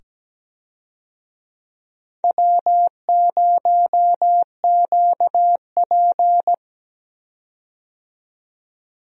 Morse code, heard as a keyed tone, about 17 wpm, 700 Hz. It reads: W0QP
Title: USA Call Signs - 4 Characters - 17wpm